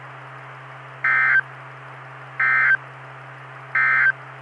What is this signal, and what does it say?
Emergency Alert System (SAME) tones, heard as harsh buzzing bursts. It is heard from 0.9 to 4.2 s.
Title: Saline, AK SKYWARN Repeater System - W5RHS